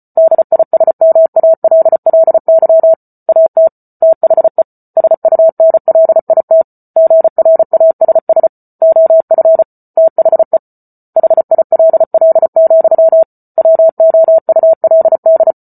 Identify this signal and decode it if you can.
Morse code, heard as a keyed tone, about 34 wpm, 650 Hz. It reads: DISMALLY AT THE SUNLIT GRASS OF THE HILL, WOULD